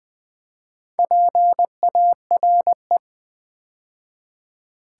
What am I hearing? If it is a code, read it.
Morse code, heard as a keyed tone, about 20 wpm, 700 Hz. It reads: PARE
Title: Build A Word - 2 Letters At A Time - 6-Letter Words - 20wpm